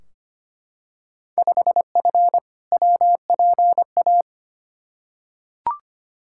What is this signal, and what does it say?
Morse code, heard as a keyed tone, about 25 wpm, 700 Hz. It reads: 5F WPA
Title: Field Day Exchanges Encore - 25wpm